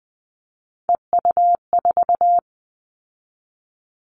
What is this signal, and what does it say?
Morse code, heard as a keyed tone, about 20 wpm, 700 Hz. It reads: EU4
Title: Callsign Prefixes for Top 100 DXCC - 20wpm